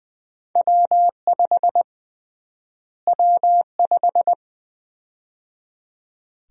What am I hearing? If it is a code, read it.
Morse code, heard as a keyed tone, about 20 wpm, 700 Hz. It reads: W5 W5
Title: USA Call Sign Prefixes - Repeated 2x - 20wpm